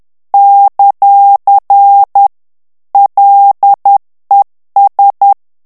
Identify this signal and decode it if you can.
Morse code, heard as a keyed tone, about 10 wpm, 800 Hz. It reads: ; LES
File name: morse mp3 / FRANCAIS TEXTE CLAIR 2 VITESSE 10 MOTS MINUTES